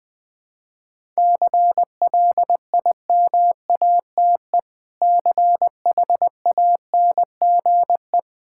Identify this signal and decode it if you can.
Morse code, heard as a keyed tone, about 20 wpm, 700 Hz. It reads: CLIMATE CHANGE
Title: Difficult Sentences with Hints - 20wpm